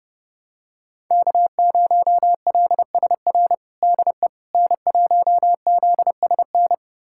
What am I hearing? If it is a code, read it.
Morse code, heard as a keyed tone, about 31 wpm, 700 Hz. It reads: K0LSR DE N1ZSN
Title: USA Callsign DE Callsign - 5 Characters - 30wpm